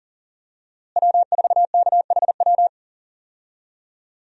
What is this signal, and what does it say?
Morse code, heard as a keyed tone, about 40 wpm, 700 Hz. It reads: W4KHW
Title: USA Call Signs - 5 Characters - Repeated 2x - 40wpm